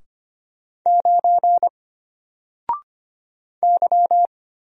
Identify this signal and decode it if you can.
Morse code, heard as a keyed tone, about 25 wpm, 700 Hz. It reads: 9 Y
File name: Single-Letter-Number-25wpm